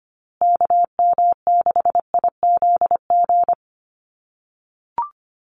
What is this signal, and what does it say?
Morse code, heard as a keyed tone, about 25 wpm, 700 Hz. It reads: KM6IZG
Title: USA Call Signs Repeated 3x 25wpm